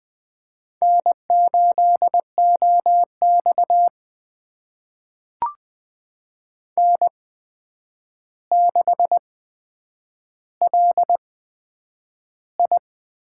Morse code, heard as a keyed tone, about 20 wpm, spaced as for 9 wpm, 700 Hz. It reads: N8OX N6LI
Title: USA Call Signs - 4 Characters - 8x Character Spacing - 20wpm